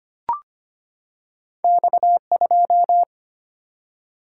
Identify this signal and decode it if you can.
Morse code, heard as a keyed tone, about 25 wpm, 700 Hz. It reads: X2
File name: Two Letter-Number Combinations - 25wpm